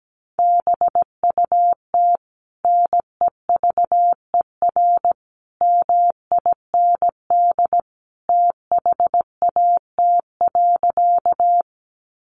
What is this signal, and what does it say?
Morse code, heard as a keyed tone, about 17 wpm, 700 Hz. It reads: BUT NEVER MIND THAT.
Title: Sentences from Top 500 Words 17wpm